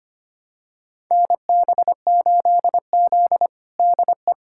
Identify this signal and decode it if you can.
Morse code, heard as a keyed tone, about 25 wpm, 700 Hz. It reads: NB8Z DE
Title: USA Callsign DE Callsign - 4 Characters - 25wpm